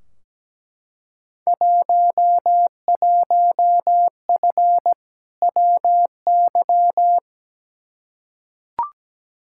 Morse code, heard as a keyed tone, about 17 wpm, 700 Hz. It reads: 11F WY
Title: Field Day Exchanges Encore - 17wpm